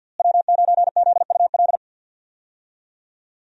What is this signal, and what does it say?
Morse code, heard as a keyed tone, about 47 wpm, 700 Hz. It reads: W9ZUL
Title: USA Call Signs - 5 Characters - Repeated 3x - 50wpm